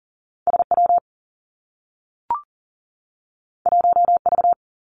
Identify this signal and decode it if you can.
Morse code, heard as a keyed tone, about 40 wpm, 700 Hz. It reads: SW 1V
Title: Two Challenging Letter-Number Combinations 40wpm